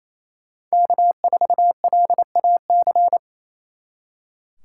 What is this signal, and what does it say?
Morse code, heard as a keyed tone, about 36 wpm, 700 Hz. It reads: K4LAC